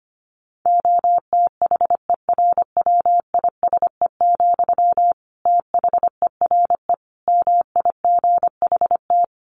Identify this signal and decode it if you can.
Morse code, heard as a keyed tone, about 25 wpm, 700 Hz. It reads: OTHERWISE, THERE MIGHT
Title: Sentences from Top 1400 Words 25wpm